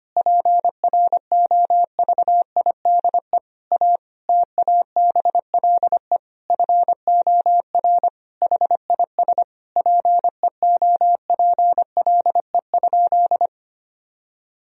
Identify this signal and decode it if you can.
Morse code, heard as a keyed tone, about 25 wpm, 700 Hz. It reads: PROVIDE A TABLE FOR HIS PEOPLE?
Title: Sentences from Top 1000 Words - Review Part 3 - 25wpm